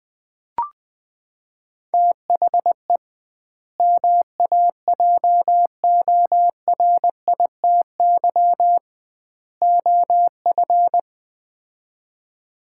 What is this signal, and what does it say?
Morse code, heard as a keyed tone, about 20 wpm, 700 Hz. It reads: THE MAJORITY OF
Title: 2x Word-Spacing - Sets of 3 Words - 20wpm